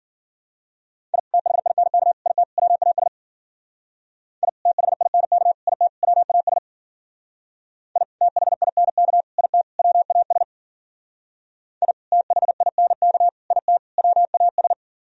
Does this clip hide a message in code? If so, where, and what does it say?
Yes: Morse code from 1.1 to 14.7 s, heard as a keyed tone, about 51 wpm, 700 Hz. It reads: I THINK IT WAS I THINK IT WAS I THINK IT WAS I THINK IT WAS